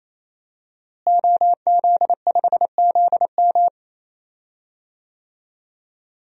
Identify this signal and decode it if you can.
Morse code, heard as a keyed tone, about 27 wpm, 700 Hz. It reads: OZ5ZM
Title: DXCC-100 Call Signs - 5 Characters - Repeated 3x - 28wpm